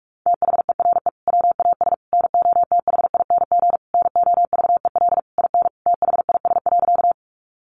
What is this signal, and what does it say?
Morse code, heard as a keyed tone, about 44 wpm, 700 Hz. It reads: THERE WAS NOTHING NOVEL IN THIS.
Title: Sentences from Top 1300 Words 45wpm